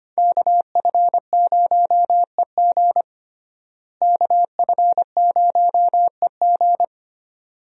Morse code, heard as a keyed tone, about 25 wpm, 700 Hz. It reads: KF0EG KF0EG
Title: USA Call Signs - 5 Characters - Repeated 2x - 25wpm